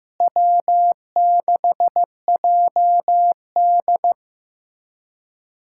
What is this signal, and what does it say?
Morse code, heard as a keyed tone, about 15 wpm, 700 Hz. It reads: W6JD
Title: USA Callsign DE Callsign - 4 Characters - 15wpm